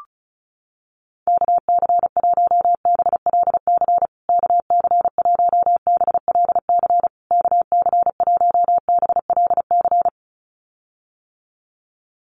Morse code, heard as a keyed tone, about 34 wpm, 700 Hz. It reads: KC1BLC KC1BLC KC1BLC
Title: USA Call Signs Repeated 3x 35wpm